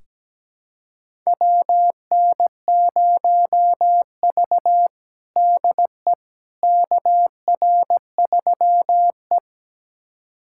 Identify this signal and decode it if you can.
Morse code, heard as a keyed tone, about 17 wpm, 700 Hz. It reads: WN0V DE KR3E
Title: USA Callsign DE Callsign - 4 Characters - 17wpm